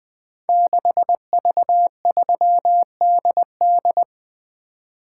Morse code, heard as a keyed tone, about 20 wpm, 700 Hz. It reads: 6V3DD
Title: Code Groups - 5 Letter:Numbers - 20wpm